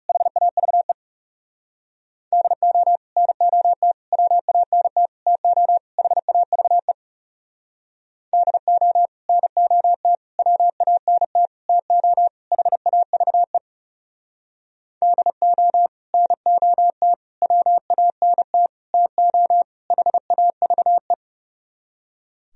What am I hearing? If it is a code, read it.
Morse code, heard as a keyed tone, about 34 wpm, 700 Hz. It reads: HAVE DO NOT WANT TO HAVE DO NOT WANT TO HAVE DO NOT WANT TO HAVE
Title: Speed-Racing - Sets of 5 Words 30wpm